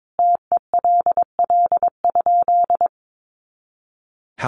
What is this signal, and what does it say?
Morse code, heard as a keyed tone, about 22 wpm, 700 Hz. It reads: TELL?
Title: Sentences from Top 200 Words 22wpm